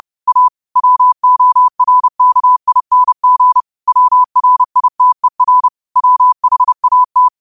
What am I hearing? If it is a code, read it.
Morse code, heard as a keyed tone, about 30 wpm, 1000 Hz. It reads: A WORKING WRITER WHAT
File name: qotdfiles / 30wpm